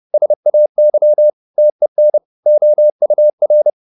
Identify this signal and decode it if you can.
Morse code, heard as a keyed tone, about 30 wpm, 600 Hz. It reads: SAY TEN OUR
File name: the old man and the sea / the old man and the sea 1 30wpm